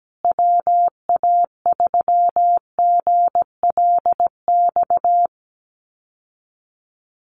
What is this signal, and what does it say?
Morse code, heard as a keyed tone, about 17 wpm, 700 Hz. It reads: WA3GLX
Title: USA Call Signs Repeated 2x 17wpm